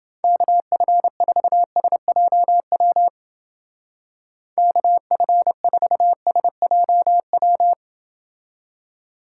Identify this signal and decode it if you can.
Morse code, heard as a keyed tone, about 30 wpm, 700 Hz. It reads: KF4SJW KF4SJW